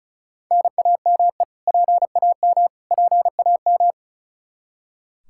Morse code, heard as a keyed tone, about 36 wpm, 700 Hz. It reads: NAME PAM PAM
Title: Practice QSO - Exchange 1 - Beginning - 35wpm